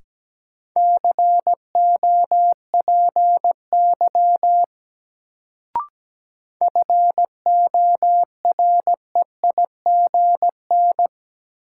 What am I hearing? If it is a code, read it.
Morse code, heard as a keyed tone, about 17 wpm, 700 Hz. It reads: COPY FOREIGN
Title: Sentences from Top 1400 Words - New Word Drill Alt - 17wpm